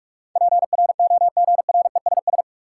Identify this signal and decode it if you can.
Morse code, heard as a keyed tone, about 41 wpm, 700 Hz. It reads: PROGRESS
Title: Sentences from Top 1400 Words - New Word Drill - 45wpm